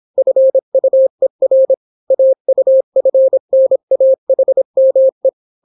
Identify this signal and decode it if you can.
Morse code, heard as a keyed tone, about 26 wpm, 530 Hz. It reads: FUER AUFNAHME